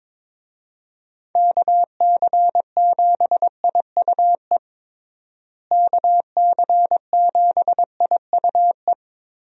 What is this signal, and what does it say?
Morse code, heard as a keyed tone, about 22 wpm, 700 Hz. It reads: KC7IUE KC7IUE
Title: USA Call Signs - 6 Characters - Repeated 3x - 22wpm